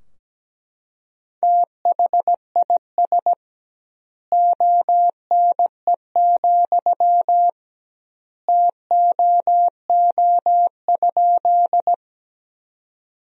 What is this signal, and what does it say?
Morse code, heard as a keyed tone, about 17 wpm, 700 Hz. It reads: THIS ONE, TOO?
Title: 2x Word-Spacing - Sentences from Top 200 Words - 17wpm